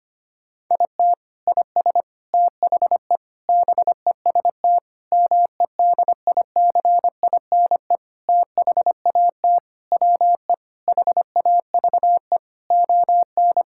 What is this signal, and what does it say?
Morse code, heard as a keyed tone, about 25 wpm, 700 Hz. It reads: IT IS THE BEST MEDICINE THAT WE HAVE ON
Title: Sentences from Top 1000 Words - Review Part 6 - 25wpm